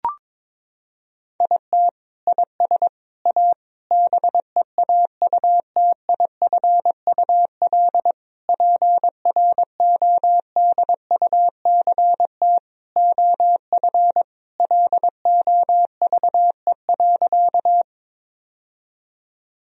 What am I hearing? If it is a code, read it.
Morse code, heard as a keyed tone, about 22 wpm, 700 Hz. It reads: IT IS A BEAUTIFUL PRODUCT OF LOVE.
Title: Sentences from Top 1000 Words - Review Part 2 - 22wpm